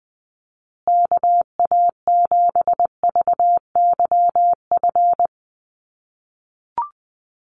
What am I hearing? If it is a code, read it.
Morse code, heard as a keyed tone, about 20 wpm, 700 Hz. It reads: KA7VYF
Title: USA Call Signs Repeated 3x 20wpm